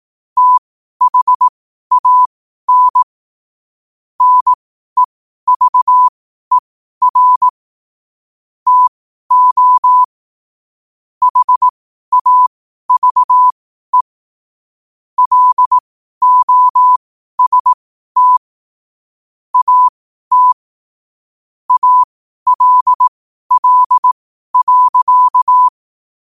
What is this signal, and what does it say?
Morse code, heard as a keyed tone, about 18 wpm, spaced as for 12 wpm, 1000 Hz. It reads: THAN NEVER TO HAVE LOST AT ALL.
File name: qotdfiles / 12wpm